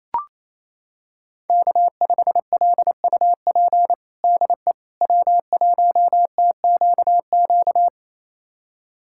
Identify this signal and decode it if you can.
Morse code, heard as a keyed tone, about 27 wpm, 700 Hz. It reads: K5LUP DE W1TQQ
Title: USA Callsign DE Callsign - 5 Characters - 28wpm